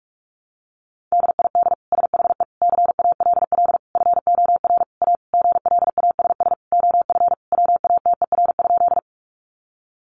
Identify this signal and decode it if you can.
Morse code, heard as a keyed tone, about 44 wpm, 700 Hz. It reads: DID SHE CALL FOR A GLASS OF WATER?
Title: Sentences from Top 1300 Words 45wpm